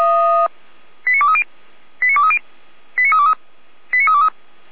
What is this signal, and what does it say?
Five-tone selective call (ZVEI), heard as quick runs of five tones, tones of about 70 ms. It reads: 89230 89230 89323 89323